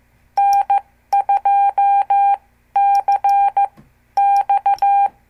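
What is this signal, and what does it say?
Morse code, heard as a keyed tone, about 14 wpm, spaced as for 11 wpm, 800 Hz. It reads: N2CX
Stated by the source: Example CW QSO between N2APB and N2CX from a CWTD Morse lesson.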